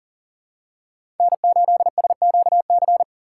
Morse code, heard as a keyed tone, about 40 wpm, 700 Hz. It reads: N8SQC
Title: USA Call Signs - 5 Characters - 2x Character Spacing - 40wpm